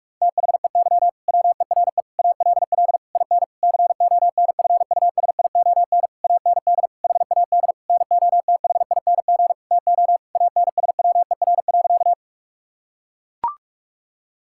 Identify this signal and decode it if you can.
Morse code, heard as a keyed tone, about 42 wpm, 700 Hz. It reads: THEY WERE ALL IN CONFUSION AND HAD NOTHING TO ANSWER.
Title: Sentences from Top 1600 Words 45wpm